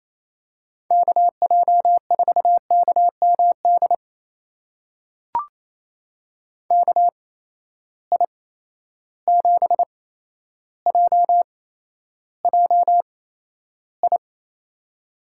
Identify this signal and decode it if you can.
Morse code, heard as a keyed tone, about 28 wpm, spaced as for 12 wpm, 700 Hz. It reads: KJ4KMD KI7JJI
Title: USA Call Signs - 6 Characters - 8x Character Spacing - 28wpm